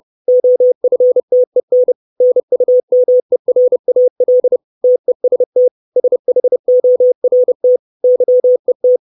Morse code, heard as a keyed tone, about 30 wpm, 500 Hz. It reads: OFTEN NUMERAL TEST SHORT YET
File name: 632 Words 30WPM